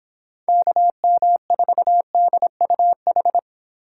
Morse code, heard as a keyed tone, about 26 wpm, 700 Hz. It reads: KM4DUH
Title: Speed Racing - USA 2x3 Call Signs 22wpm